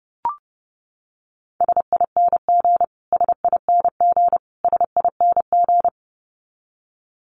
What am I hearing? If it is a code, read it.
Morse code, heard as a keyed tone, about 30 wpm, 700 Hz. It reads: SING SING SING
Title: Top 500 Words Repeated 30wpm